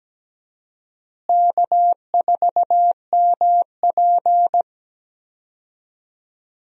Morse code, heard as a keyed tone, about 17 wpm, 700 Hz. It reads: K4MP
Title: USA Call Signs - 4 Characters - Repeated 3x - 17wpm